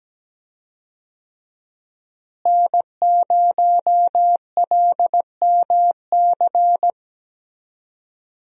Morse code, heard as a keyed tone, about 17 wpm, 700 Hz. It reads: N0LMC
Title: USA Call Signs - 5 Characters - Repeated 3x - 17wpm